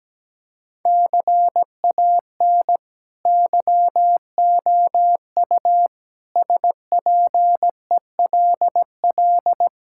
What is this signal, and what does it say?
Morse code, heard as a keyed tone, about 17 wpm, 700 Hz. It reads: CAN YOU SPELL
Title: Sentences from Top 1000 Words - Review Part 5 - 17wpm